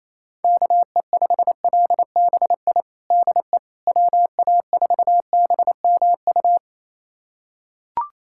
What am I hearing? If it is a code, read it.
Morse code, heard as a keyed tone, about 28 wpm, 700 Hz. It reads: KE5LBI DE WA4BMU
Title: USA Callsign DE Callsign - 6 Characters - 28wpm